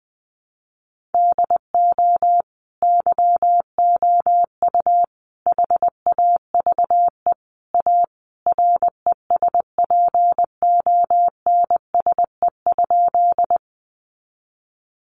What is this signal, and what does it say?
Morse code, heard as a keyed tone, about 20 wpm, 700 Hz. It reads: DO YOU HAVE A RESPONSE?
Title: Sentences from Top 1400 Words 20wpm